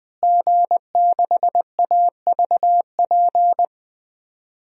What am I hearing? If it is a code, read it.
Morse code, heard as a keyed tone, about 20 wpm, 700 Hz. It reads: G6AVP
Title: DXCC-100 Call Signs - 5 Characters - Repeated 3x - 20wpm